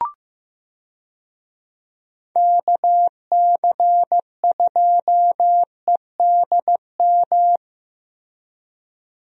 Morse code, heard as a keyed tone, about 15 wpm, 700 Hz. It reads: KC2EDM
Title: USA Call Signs - 6 Characters - Repeated 2x - 15wpm